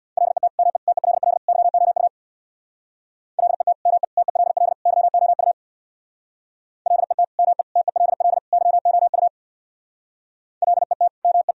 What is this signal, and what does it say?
Morse code, heard as a keyed tone, about 62 wpm, 700 Hz. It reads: LET ME TELL YOU LET ME TELL YOU LET ME TELL YOU LET ME
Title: Speed-Racing - Sets of 4 Words 50wpm